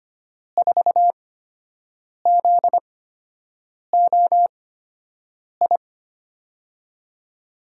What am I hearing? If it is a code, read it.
Morse code, heard as a keyed tone, about 25 wpm, spaced as for 7 wpm, 700 Hz. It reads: 4ZOI
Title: USA Call Signs - 6 Characters - 8x Character Spacing - 25wpm